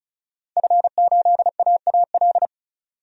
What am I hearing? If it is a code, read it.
Morse code, heard as a keyed tone, about 36 wpm, 700 Hz. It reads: F8AAL
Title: DXCC-100 Call Signs - 5 Characters - 35wpm